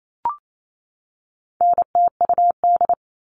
Morse code, heard as a keyed tone, about 28 wpm, 700 Hz. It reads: NTUD
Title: All Cut Numbers - 4-Digit 28wpm